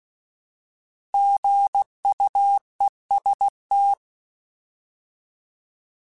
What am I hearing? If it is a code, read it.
Morse code, heard as a keyed tone, about 16 wpm, 800 Hz. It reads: GUEST
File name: Alphabetical / Alphabetical mp3s / Gs @15